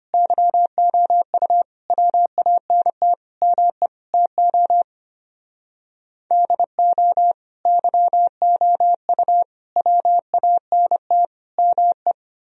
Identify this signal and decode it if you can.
Morse code, heard as a keyed tone, about 32 wpm, 700 Hz. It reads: YOU WANT ME TO DO YOU WANT ME